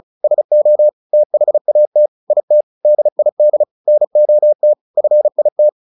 Morse code, heard as a keyed tone, about 34 wpm, 600 Hz. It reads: SO THAT IT DID NOT FIT
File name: the old man and the sea 2 35wpm